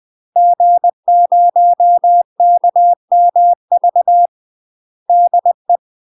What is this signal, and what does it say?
Morse code, heard as a keyed tone, about 20 wpm, 700 Hz. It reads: G0KMV DE
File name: UK MASTER QSOs TRAINING / 20WPM / UK MASTER QSOs 0037